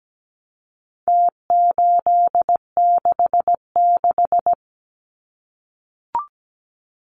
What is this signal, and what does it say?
Morse code, heard as a keyed tone, about 17 wpm, 700 Hz. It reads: T866
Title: Numbers - Contest Style - 4-Digit 17wpm